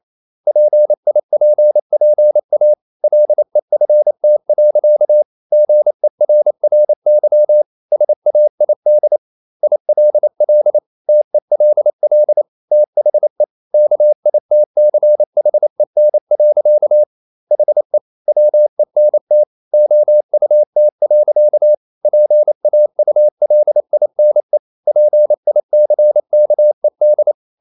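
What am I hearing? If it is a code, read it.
Morse code, heard as a keyed tone, about 28 wpm, 600 Hz. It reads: PIPPA LEFT. GERRY SAID ILL TELL THE KITCHEN. HE WENT OUT. PAULINE PICKED
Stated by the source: Morse code practice at 28 wpm, text of Ken Follett's novel Never in English, from CW med Gnister.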